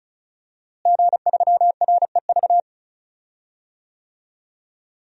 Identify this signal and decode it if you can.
Morse code, heard as a keyed tone, about 34 wpm, 700 Hz. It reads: G3REV
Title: DXCC-100 Call Signs - 5 Characters - Repeated 3x - 35wpm